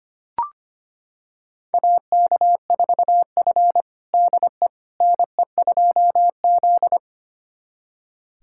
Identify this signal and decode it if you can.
Morse code, heard as a keyed tone, about 25 wpm, 700 Hz. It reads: AK4F DE NE2Z
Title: USA Callsign DE Callsign - 4 Characters - 25wpm